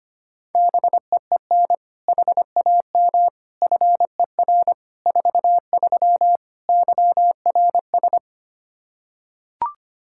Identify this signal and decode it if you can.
Morse code, heard as a keyed tone, about 25 wpm, 700 Hz. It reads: BEEN HAM FER 43 YRS
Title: Practice QSO - Exchange 3 - More Info - 25wpm